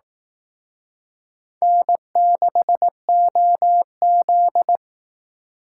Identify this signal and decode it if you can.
Morse code, heard as a keyed tone, about 18 wpm, 700 Hz. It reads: N6OZ
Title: Speed Racing - USA 1x2 Call Signs 15wpm